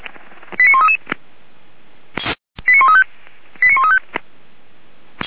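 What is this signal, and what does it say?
Five-tone selective call (ZVEI), heard as quick runs of five tones, tones of about 80 ms. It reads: 89133 89135 89135
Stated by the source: Five-tone ZVEI selective call 89138 received by SDR, 464 channel, from Feuerwehr Orlamünde fire brigade server.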